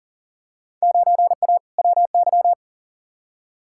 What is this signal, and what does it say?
Morse code, heard as a keyed tone, about 40 wpm, 700 Hz. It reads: 9A WY
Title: Field Day Exchanges Encore - 40wpm